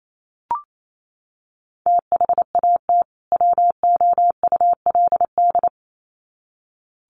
Morse code, heard as a keyed tone, about 28 wpm, 700 Hz. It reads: THAT WOULD
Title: Sets of 2 Words 28wpm